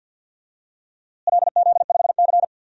Morse code, heard as a keyed tone, about 52 wpm, 700 Hz. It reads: LZ5C
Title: DXCC-100 Call Signs - 4 Characters - Repeated 3x - 50wpm